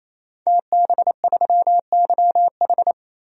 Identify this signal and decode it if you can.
Morse code, heard as a keyed tone, about 28 wpm, 700 Hz. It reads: TB3YH
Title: DXCC-100 Call Signs - 5 Characters - 28wpm